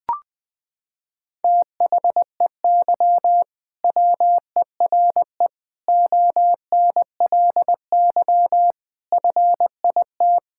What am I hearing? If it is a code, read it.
Morse code, heard as a keyed tone, about 20 wpm, 700 Hz. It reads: THEY WERE ONLY FIT
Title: Sentences from Top 1000 Words - Review Part 5 - 20wpm